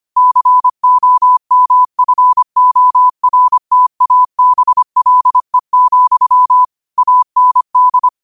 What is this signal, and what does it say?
Morse code, heard as a keyed tone, about 25 wpm, 1000 Hz. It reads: COMFORTABLE, AND
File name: qotdfiles / 25wpm